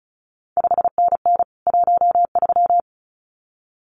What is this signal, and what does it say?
Morse code, heard as a keyed tone, about 36 wpm, 700 Hz. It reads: 5NN 13
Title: CQ WW DX Contest Exchange Encore - 35wpm